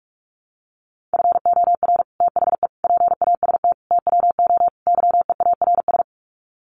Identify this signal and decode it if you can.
Morse code, heard as a keyed tone, about 44 wpm, 700 Hz. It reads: FOR THE PAST TWO YEARS
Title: Sets of 5 Words Encore2 45wpm